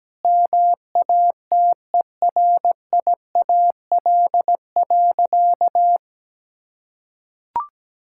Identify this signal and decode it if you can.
Morse code, heard as a keyed tone, about 17 wpm, 700 Hz. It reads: MATERIAL.
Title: Sentences from Top 1000 Words - Review Part 5 - 17wpm